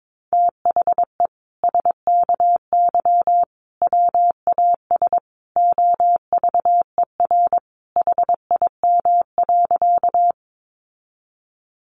Morse code, heard as a keyed tone, about 22 wpm, 700 Hz. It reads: THE SKY WAS OVER HIM.
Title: Sentences from Top 1200 Words 22wpm